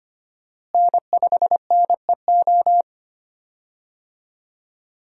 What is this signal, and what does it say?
Morse code, heard as a keyed tone, about 25 wpm, 700 Hz. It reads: N5NEO
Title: USA Call Signs - 5 Characters - Repeated 2x - 25wpm